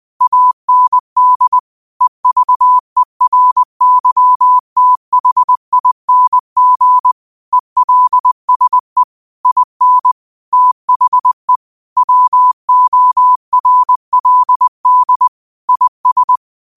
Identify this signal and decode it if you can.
Morse code, heard as a keyed tone, about 20 wpm, 1000 Hz. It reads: AND EVERYTHING ELSE IN THE WORLD IS
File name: qotdfiles / 20wpm